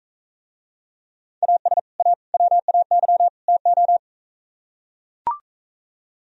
Morse code, heard as a keyed tone, about 44 wpm, 700 Hz. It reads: AS A WAY TO